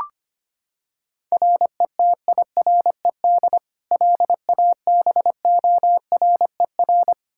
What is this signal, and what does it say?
Morse code, heard as a keyed tone, about 25 wpm, 700 Hz. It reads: RETIRED LABORER
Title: K1USN SST QSO Practice 25wpm